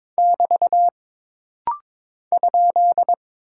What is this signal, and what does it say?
Morse code, heard as a keyed tone, about 22 wpm, 700 Hz. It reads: <BT> ?
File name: practice / 22wpm / Single Punctuation-Prosign - Rapid-Fire - 22wpm